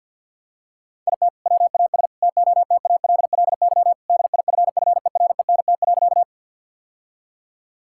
Morse code, heard as a keyed tone, about 52 wpm, 700 Hz. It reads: IT WAS TOTALLY DIFFERENT.
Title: Sentences from Top 1600 Words 50wpm